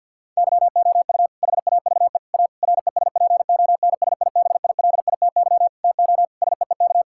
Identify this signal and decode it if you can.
Morse code, heard as a keyed tone, about 50 wpm, 700 Hz. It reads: YOU HAVE A RESPONSIBILITY TO SEEK
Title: Sentences from Top 1600 Words 50wpm